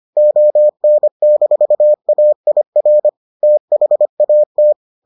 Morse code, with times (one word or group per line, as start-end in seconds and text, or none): 0.2-3.1 ON-AIR
3.4-4.7 THAT